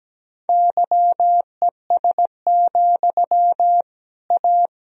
Morse code, heard as a keyed tone, about 17 wpm, 700 Hz. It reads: YES, A
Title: Sentences from Top 1000 Words - Review Part 3 - 17wpm